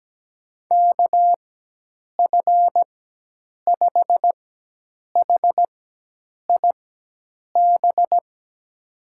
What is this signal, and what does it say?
Morse code, heard as a keyed tone, about 17 wpm, spaced as for 8 wpm, 700 Hz. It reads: KF5HIB
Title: USA Call Signs - 6 Characters - 4x Character Spacing - 17wpm